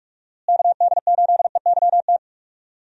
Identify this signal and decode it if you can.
Morse code, heard as a keyed tone, about 47 wpm, 700 Hz. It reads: KD8EYT
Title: USA Call Signs - 6 Characters - Repeated 3x - 45wpm